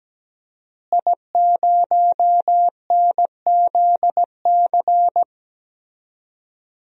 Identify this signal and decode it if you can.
Morse code, heard as a keyed tone, about 17 wpm, 700 Hz. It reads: I0NZC
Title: DXCC-100 Call Signs - 5 Characters - Repeated 3x - 17wpm